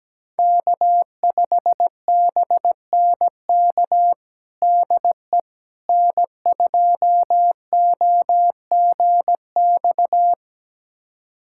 Morse code, heard as a keyed tone, about 17 wpm, 700 Hz. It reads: K5BNK DE N2OGX
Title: USA Callsign DE Callsign - 5 Characters - 17wpm